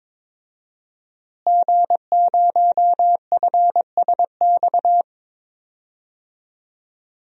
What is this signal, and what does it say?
Morse code, heard as a keyed tone, about 22 wpm, 700 Hz. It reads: G0FSX
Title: DXCC-100 Call Signs - 5 Characters - Repeated 3x - 22wpm